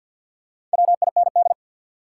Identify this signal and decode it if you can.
Morse code, heard as a keyed tone, about 52 wpm, 700 Hz. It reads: WIND
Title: Sentences from Top 400 Words - New Word Drill - 50wpm